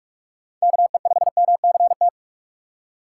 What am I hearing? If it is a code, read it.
Morse code, heard as a keyed tone, about 44 wpm, 700 Hz. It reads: KE5MCT
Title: USA Call Signs - 6x Character Spacing - 45wpm